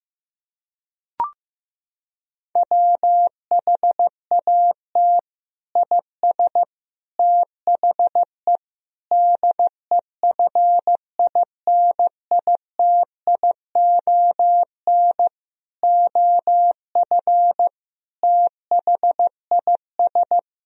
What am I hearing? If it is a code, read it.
Morse code, heard as a keyed tone, about 15 wpm, 700 Hz. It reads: WHAT IS THE DEFINITION OF THIS
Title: Sentences from Top 1000 Words - Review Part 4 - 15wpm